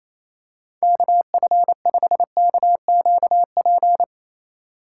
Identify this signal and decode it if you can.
Morse code, heard as a keyed tone, about 28 wpm, 700 Hz. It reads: KF5KQP
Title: USA Call Signs - 6 Characters - 4x Character Spacing - 28wpm